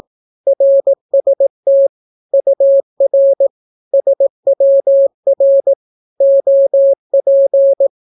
Morse code, heard as a keyed tone, about 18 wpm, 550 Hz. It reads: RST UR SWR OP
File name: naukapodstaw / skroty / skroty